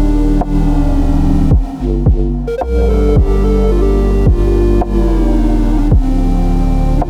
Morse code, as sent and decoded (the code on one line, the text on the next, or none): none